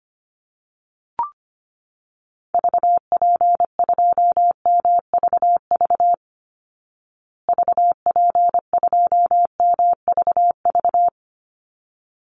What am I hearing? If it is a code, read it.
Morse code, heard as a keyed tone, about 25 wpm, 700 Hz. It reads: VP2MVV VP2MVV
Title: Call Signs Repeated 3x - 25wpm